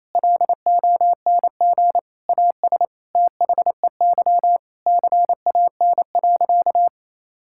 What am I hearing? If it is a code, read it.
Morse code, heard as a keyed tone, about 28 wpm, 700 Hz. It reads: LONG AS THEY CAN.
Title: Sentences from Top 1000 Words - Review Part 5 - 28wpm